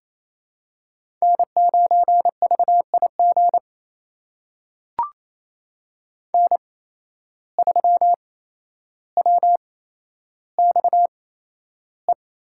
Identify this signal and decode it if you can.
Morse code, heard as a keyed tone, about 28 wpm, spaced as for 12 wpm, 700 Hz. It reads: N9VIG N3WXE
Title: USA Call Signs - 5 Characters - 8x Character Spacing - 28wpm